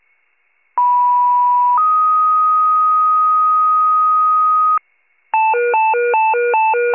Two-tone page, heard as a one-second tone, then a three-second tone, about 980 then 1250 Hz. It is heard from 0.8 to 4.8 s.
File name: wcalert